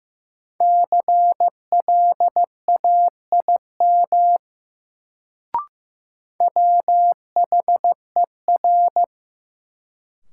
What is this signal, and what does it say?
Morse code, heard as a keyed tone, about 15 wpm, 700 Hz. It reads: CLAIM WHER
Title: Sentences from Top 1500 Words - New Word Drill - 15wpm